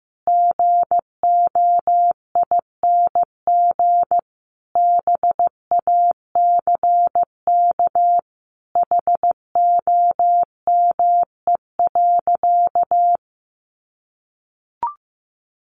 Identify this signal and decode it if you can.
Morse code, heard as a keyed tone, about 15 wpm, 700 Hz. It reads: GOING BACK HOME.
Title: Sentences from Top 500 Words 15wpm